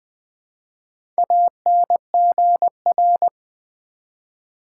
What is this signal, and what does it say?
Morse code, heard as a keyed tone, about 20 wpm, 700 Hz. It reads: ANGR
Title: Build A Word - 1 Letter At A Time - 5-Letter Words - 20wpm